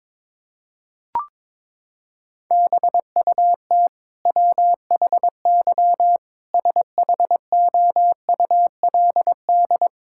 Morse code, heard as a keyed tone, about 22 wpm, 700 Hz. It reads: BUT WHY SHOULD
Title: Sentences from Top 1000 Words - Review Part 6 - 22wpm